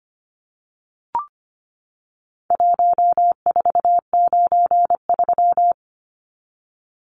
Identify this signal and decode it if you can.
Morse code, heard as a keyed tone, about 25 wpm, 700 Hz. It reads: 1493
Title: Numbers - Contest Style - 4-Digit 25wpm